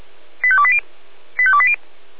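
Five-tone selective call (ZVEI), heard as quick runs of five tones, tones of about 70 ms. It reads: 85280 85280
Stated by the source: ZVEI five-tone call 85264 received by SDR, on the Feuerwehr Orlamuende audio server.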